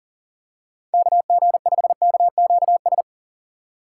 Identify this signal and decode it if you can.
Morse code, heard as a keyed tone, about 38 wpm, 700 Hz. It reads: KG5KQS
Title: USA Call Signs - 6x Character Spacing - 40wpm